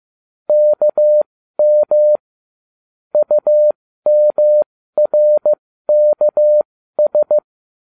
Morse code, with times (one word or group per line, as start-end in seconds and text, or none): 0.5-2.2 KM
3.1-7.4 UMRKS